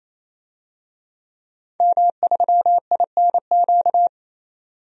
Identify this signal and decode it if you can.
Morse code, heard as a keyed tone, about 28 wpm, 700 Hz. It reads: M3INQ
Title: DXCC-100 Call Signs - 5 Characters - 28wpm